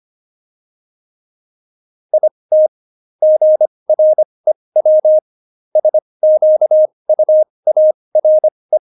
Morse code, heard as a keyed tone, about 25 wpm, spaced as for 20 wpm, 620 Hz. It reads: IT GREW SQUARE
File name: Jokes-25-char-20-eff